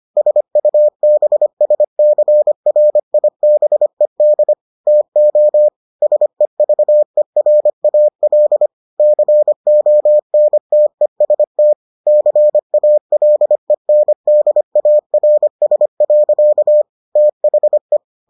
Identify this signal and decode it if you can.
Morse code, heard as a keyed tone, about 25 wpm, 600 Hz. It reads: SUBSCRIBED TO SEVERAL CONTEST CALENDARS. THE